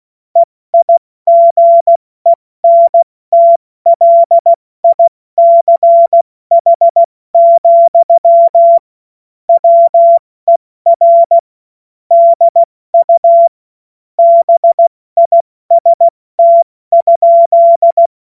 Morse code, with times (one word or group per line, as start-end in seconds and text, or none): 0.4-8.8 EIGENTLICH,
9.5-11.4 WER
12.1-13.5 DU
14.2-18.1 BIST?